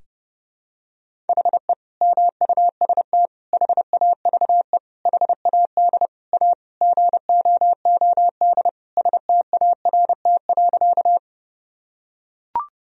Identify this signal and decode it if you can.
Morse code, heard as a keyed tone, about 32 wpm, 700 Hz. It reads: HE MUST HAVE HAD A GOOD START.
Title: Sentences from Top 1000 Words - Review Part 1 - 30wpm